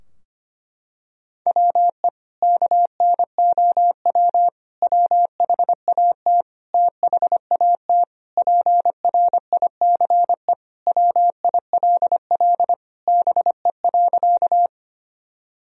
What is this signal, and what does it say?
Morse code, heard as a keyed tone, about 25 wpm, 700 Hz. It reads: WE KNOW WHAT THAT PRICE WILL BE.
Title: Sentences from Top 1000 Words - Review Part 4 - 25wpm